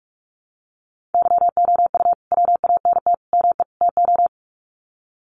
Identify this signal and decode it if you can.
Morse code, heard as a keyed tone, about 46 wpm, 700 Hz. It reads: YOU WANT ME TO
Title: Sets of 4 Words 45wpm